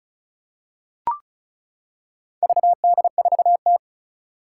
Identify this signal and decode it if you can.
Morse code, heard as a keyed tone, about 36 wpm, 700 Hz. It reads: VD4T
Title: Four Challenging Letter-Number Combinations 35wpm